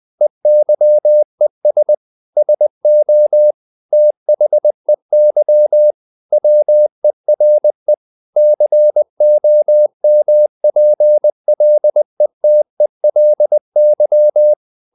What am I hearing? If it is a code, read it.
Morse code, heard as a keyed tone, about 20 wpm, 600 Hz. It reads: EYES SO THEY WERE COMPLETELY
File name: the old man and the sea / the old man and the sea part 1 20wpm